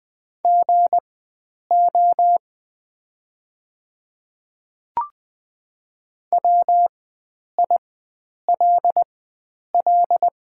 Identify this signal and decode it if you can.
Morse code, heard as a keyed tone, about 20 wpm, spaced as for 9 wpm, 700 Hz. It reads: GO WILL
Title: 4x Character Spacing - Top 100 Words - 20wpm